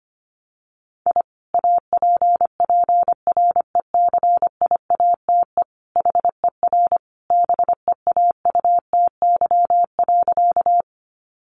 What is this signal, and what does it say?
Morse code, heard as a keyed tone, about 25 wpm, 700 Hz. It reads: I APPRECIATE HER BEAUTY.
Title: Sentences from Top 1300 Words 25wpm